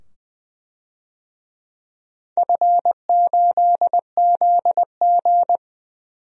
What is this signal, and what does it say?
Morse code, heard as a keyed tone, about 20 wpm, 700 Hz. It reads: F8ZG
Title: DXCC-100 Call Signs - 4 Characters - 20wpm